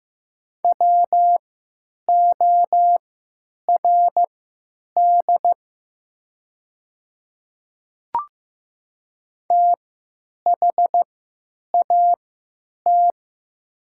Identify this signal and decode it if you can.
Morse code, heard as a keyed tone, about 15 wpm, spaced as for 8 wpm, 700 Hz. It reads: WORD THAT
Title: 3x Character Spacing - Top 100 Words - 15wpm